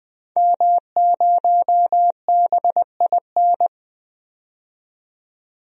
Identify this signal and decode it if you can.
Morse code, heard as a keyed tone, about 20 wpm, 700 Hz. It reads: M0BIN
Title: DXCC-100 Call Signs - 5 Characters - 20wpm